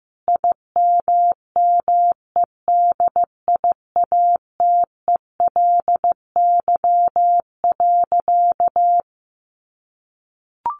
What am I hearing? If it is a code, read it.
Morse code, heard as a keyed tone, about 15 wpm, 700 Hz. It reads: IMMEDIATELY.
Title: Sentences from Top 1100 Words 15wpm